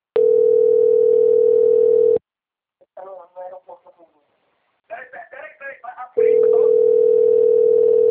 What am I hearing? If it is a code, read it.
Ringback tone (North America), 440 + 480 Hz.
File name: rg-1600-unknown- - - .44203